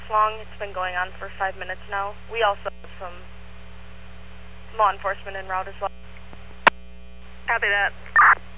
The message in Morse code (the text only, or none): none